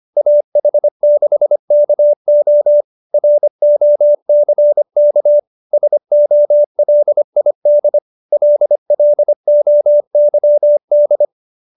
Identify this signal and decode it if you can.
Morse code, heard as a keyed tone, about 25 wpm, 570 Hz. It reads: AH6KO ROCK SOLID LLOYD